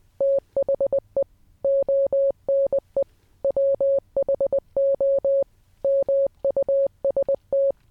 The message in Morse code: THE ONE WHO MUST